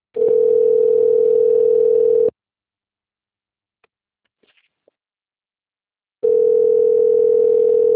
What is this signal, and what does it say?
Ringback tone (North America), 440 + 480 Hz.